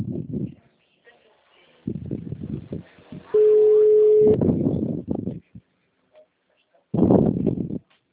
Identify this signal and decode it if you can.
Ringback tone (Europe, 425 Hz), with speech.
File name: out- -2046- - - .85328